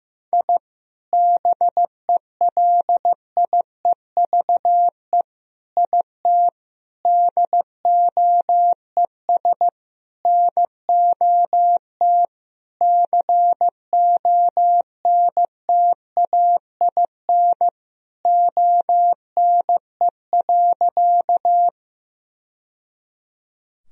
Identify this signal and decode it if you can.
Morse code, heard as a keyed tone, about 15 wpm, 700 Hz. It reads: I BELIEVE IT DOES NOT CONTAIN ONE.
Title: Sentences from Top 1000 Words - Review Part 2 - 15wpm